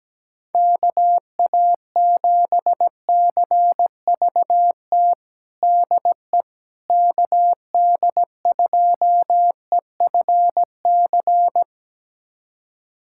Morse code, heard as a keyed tone, about 17 wpm, 700 Hz. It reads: KA7CVT DE KD2EFC
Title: USA Callsign DE Callsign 17wpm